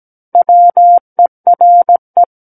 Morse code, heard as a keyed tone, about 17 wpm, 700 Hz. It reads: WERE
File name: Book One / 08-Chapter 8 - 17 WPM